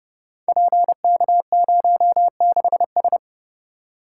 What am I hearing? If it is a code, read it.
Morse code, heard as a keyed tone, about 28 wpm, 700 Hz. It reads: PK06S
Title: Code Groups - 5 Letter:Numbers - 30wpm